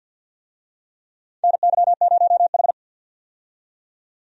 Morse code, heard as a keyed tone, about 52 wpm, 700 Hz. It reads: NY0H
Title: USA Call Signs - 4 Characters - 50wpm